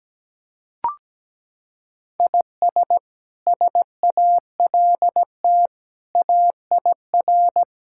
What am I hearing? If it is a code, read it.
Morse code, heard as a keyed tone, about 17 wpm, 700 Hz. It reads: IS SALT AIR
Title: Sentences from Top 1000 Words - Review Part 5 - 17wpm